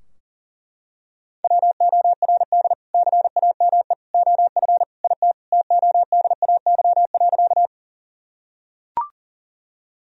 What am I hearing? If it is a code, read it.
Morse code, heard as a keyed tone, about 37 wpm, 700 Hz. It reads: WORD CAME OF IT TODAY.
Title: Sentences from Top 1000 Words - Review Part 4 - 40wpm